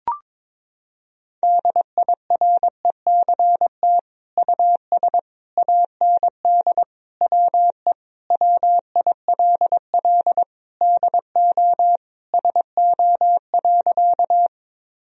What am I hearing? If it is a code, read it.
Morse code, heard as a keyed tone, about 22 wpm, 700 Hz. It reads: DIRECT US AND WE WILL DO SO.
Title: Sentences from Top 1000 Words - Review Part 2 - 22wpm